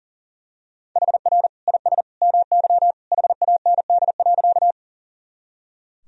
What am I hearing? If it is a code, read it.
Morse code, heard as a keyed tone, about 40 wpm, 700 Hz. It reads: HR IS MY HAND.
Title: Sentences from Top 200 Words and Abbreviations 40wpm